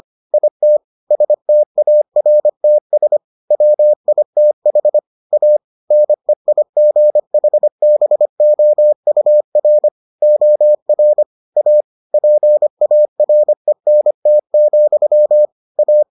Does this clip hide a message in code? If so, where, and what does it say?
Yes: Morse code from 0.3 to 16.0 s, heard as a keyed tone, about 25 wpm, 600 Hz. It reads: IT STARTS WITH A NEIGHBOUR OR A PARENT, A